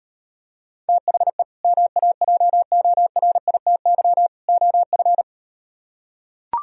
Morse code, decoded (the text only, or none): THE MAJORITY OF